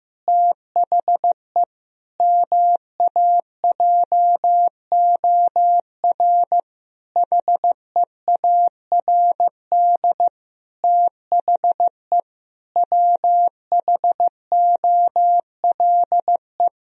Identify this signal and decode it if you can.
Morse code, heard as a keyed tone, about 15 wpm, 700 Hz. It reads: THE MAJOR HEARD THE WHOLE
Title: Sentences from Top 1000 Words - Review Part 4 - 15wpm